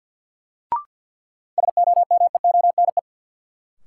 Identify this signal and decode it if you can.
Morse code, heard as a keyed tone, about 52 wpm, 700 Hz. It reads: SOMEONE
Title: Sentences from Top 600 Words - New Word Drill - 50wpm